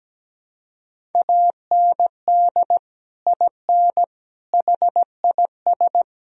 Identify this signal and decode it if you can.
Morse code, heard as a keyed tone, about 17 wpm, 700 Hz. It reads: AND IN HIS
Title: Sentences from Top 1000 Words - Review Part 1 - 17wpm